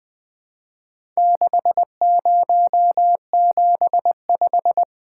Morse code, heard as a keyed tone, about 20 wpm, 700 Hz. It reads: 6075
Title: Four Letter-Number Combinations - 20wpm